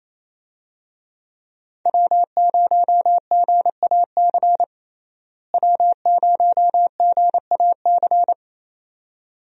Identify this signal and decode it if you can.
Morse code, heard as a keyed tone, about 28 wpm, 700 Hz. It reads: W0GAC W0GAC
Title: USA Call Signs - 5 Characters - Repeated 2x - 28wpm